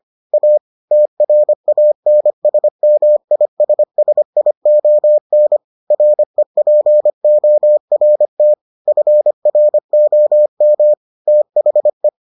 Morse code, heard as a keyed tone, about 25 wpm, 600 Hz. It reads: A TRANSMISSION REPORT FROM THE